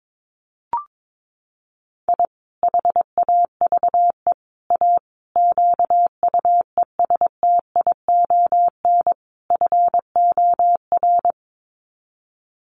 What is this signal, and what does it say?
Morse code, heard as a keyed tone, about 22 wpm, 700 Hz. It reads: I HAVE A QUESTION FOR
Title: Sets of 5 Words Encore2 22wpm